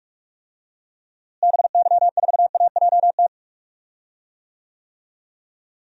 Morse code, heard as a keyed tone, about 45 wpm, 700 Hz. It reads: BY4AJT
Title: DXCC-100 Call Signs - 6 Characters - Repeated 2x - 45wpm